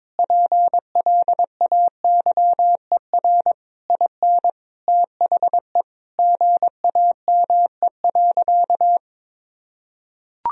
Morse code, heard as a keyed tone, about 22 wpm, 700 Hz. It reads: PLAYER IN THE GAME.
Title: Sentences from Top 1000 Words - Review Part 4 - 22wpm